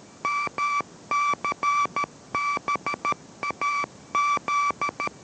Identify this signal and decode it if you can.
Morse code, heard as a keyed tone, about 13 wpm, 1160 Hz. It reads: MCBAG